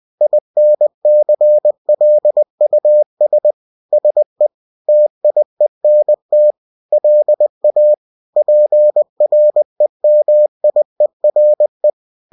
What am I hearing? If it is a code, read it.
Morse code, heard as a keyed tone, about 20 wpm, 600 Hz. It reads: INCLUS SE TIENT LA PREMIERE